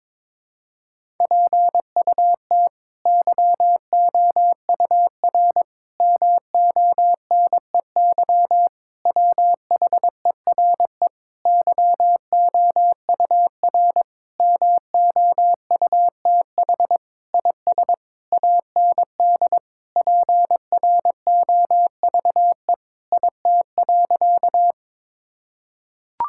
Morse code, heard as a keyed tone, about 22 wpm, 700 Hz. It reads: PUT YOUR MONEY WHERE YOUR MOUTH IS AND PROVE IT.
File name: practice / 22wpm / Sentences with 8-Word Phrases - 22wpm